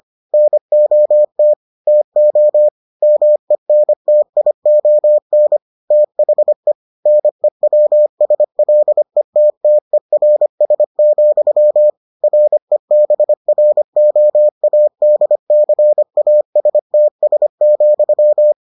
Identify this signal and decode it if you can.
Morse code, heard as a keyed tone, about 25 wpm, 600 Hz. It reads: NOT TO MENTION THE NEWSLETTERS, REBROADCASTS,